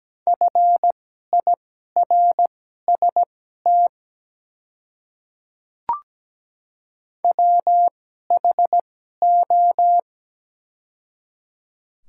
Morse code, heard as a keyed tone, about 17 wpm, spaced as for 12 wpm, 700 Hz. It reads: FIRST WHO
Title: 2x Character Spacing - Top 100 Words - 17wpm